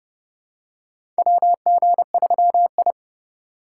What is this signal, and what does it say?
Morse code, heard as a keyed tone, about 30 wpm, 700 Hz. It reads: WG3I
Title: USA Call Signs - 4x Character Spacing - 30wpm